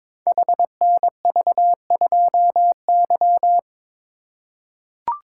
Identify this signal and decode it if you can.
Morse code, heard as a keyed tone, about 22 wpm, 700 Hz. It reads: HNV2Y
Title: Code Groups - 5 Letter:Numbers - 22wpm